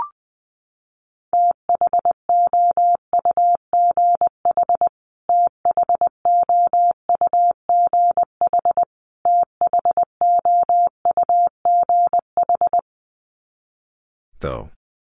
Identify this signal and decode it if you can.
Morse code, heard as a keyed tone, about 20 wpm, 700 Hz. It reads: THOUGH THOUGH THOUGH
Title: Top 500 Words Repeated 20wpm